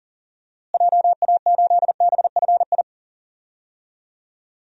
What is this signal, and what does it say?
Morse code, heard as a keyed tone, about 40 wpm, 700 Hz. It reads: JA8BFI
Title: DXCC-100 Call Signs - 6 Characters - Repeated 3x - 40wpm